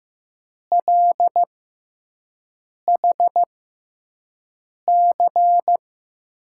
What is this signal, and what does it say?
Morse code, heard as a keyed tone, about 15 wpm, spaced as for 5 wpm, 700 Hz. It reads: LHC
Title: USA Call Signs - 5 Characters - 6x Character Spacing - 15wpm